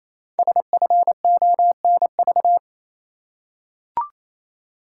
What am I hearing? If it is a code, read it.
Morse code, heard as a keyed tone, about 29 wpm, 700 Hz. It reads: SFONV
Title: Code Groups - 5 Letters - 28wpm